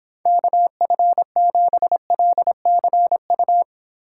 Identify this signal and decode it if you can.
Morse code, heard as a keyed tone, about 26 wpm, 700 Hz. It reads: KF7LCU